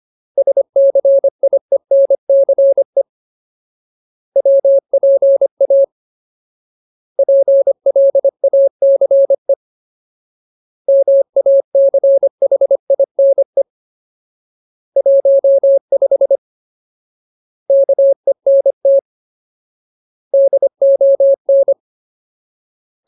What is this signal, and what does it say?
Morse code, heard as a keyed tone, about 25 wpm, 550 Hz. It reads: SCIENCE WPA PLACE MACHINE 15 KENT DON
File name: Ultimate Word List / 25WPM Ultimate